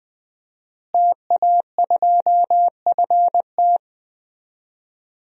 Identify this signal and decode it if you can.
Morse code, heard as a keyed tone, about 20 wpm, 700 Hz. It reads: TA2FT
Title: DXCC-100 Call Signs - 5 Characters - Repeated 2x - 20wpm